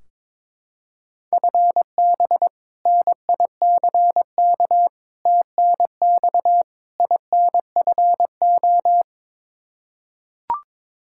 Morse code, heard as a keyed tone, about 22 wpm, 700 Hz. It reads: FB NICK TNX INFO
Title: Practice QSO - Exchange 3 - More Info - 22wpm